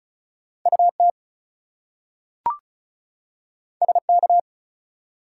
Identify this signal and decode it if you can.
Morse code, heard as a keyed tone, about 36 wpm, 700 Hz. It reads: UT SK
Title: US States and Canadian Provinces 35wpm